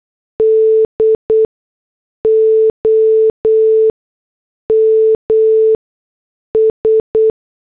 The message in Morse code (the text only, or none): DOMS